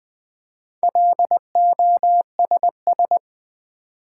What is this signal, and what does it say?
Morse code, heard as a keyed tone, about 21 wpm, 700 Hz. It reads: LOSS
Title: Four Letter Words - 20wpm